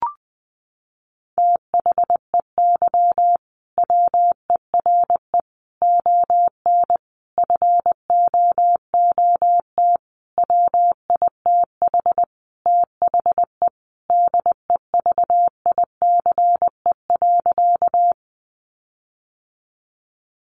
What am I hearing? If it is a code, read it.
Morse code, heard as a keyed tone, about 20 wpm, 700 Hz. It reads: THEY WERE ON FOOT WITH THE DEVICE.
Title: Sentences from Top 1500 Words 20wpm